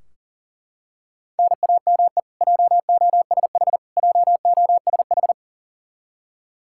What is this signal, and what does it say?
Morse code, heard as a keyed tone, about 37 wpm, 700 Hz. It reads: NAME JOSH JOSH
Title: Practice QSO - Exchange 1 - Beginning - 40wpm